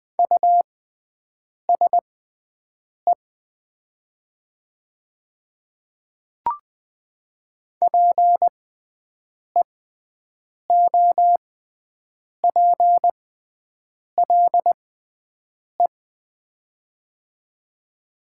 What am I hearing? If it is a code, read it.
Morse code, heard as a keyed tone, about 20 wpm, spaced as for 7 wpm, 700 Hz. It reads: USE PEOPLE
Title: 6x Character Spacing - Top 300 Words - 20wpm